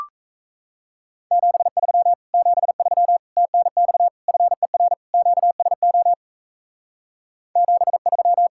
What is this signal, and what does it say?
Morse code, heard as a keyed tone, about 44 wpm, 700 Hz. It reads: 73 73 TNX FER QSO 73